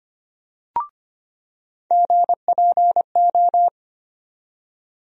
Morse code, heard as a keyed tone, about 25 wpm, 700 Hz. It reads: GPO
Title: USA Call Sign Postfixes - 3 Letters 25wpm